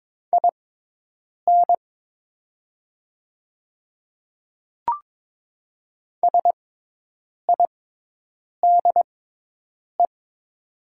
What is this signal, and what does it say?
Morse code, heard as a keyed tone, about 22 wpm, spaced as for 5 wpm, 700 Hz. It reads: INSIDE
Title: 6x Character Spacing - Top 100 Words - 22wpm